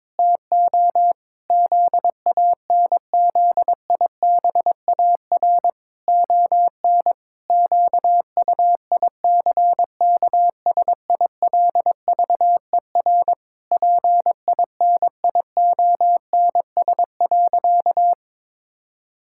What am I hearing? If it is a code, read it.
Morse code, heard as a keyed tone, about 22 wpm, 700 Hz. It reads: TO ZANZIBAR ON QUICKSILVER PINIONS.
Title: Pangram Challenge v2 - 22wpm